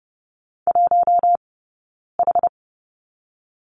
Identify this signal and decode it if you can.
Morse code, heard as a keyed tone, about 30 wpm, 700 Hz. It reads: 1 H
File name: Lesson 18d - Letter m - Two Characters 30wpm